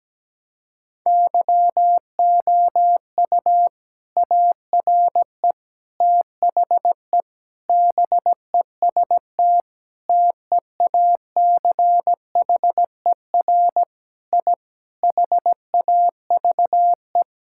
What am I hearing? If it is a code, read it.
Morse code, heard as a keyed tone, about 17 wpm, 700 Hz. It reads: YOU ARE THE BEST TEACHER I HAVE
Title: Sentences from Top 1000 Words - Review Part 6 - 17wpm